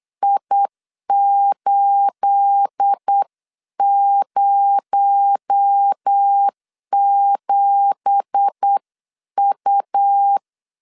Morse code, heard as a keyed tone, about 8 wpm, 800 Hz. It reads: I807U
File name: 8wpm